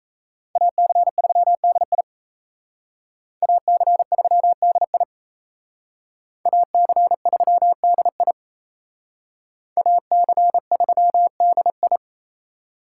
Morse code, heard as a keyed tone, about 37 wpm, 700 Hz. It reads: AC3DI AC3DI AC3DI AC3DI